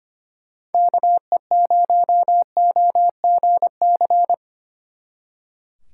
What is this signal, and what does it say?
Morse code, heard as a keyed tone, about 30 wpm, 700 Hz. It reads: KE0OGC
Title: Speed Racing - USA 2x3 Call Signs 25wpm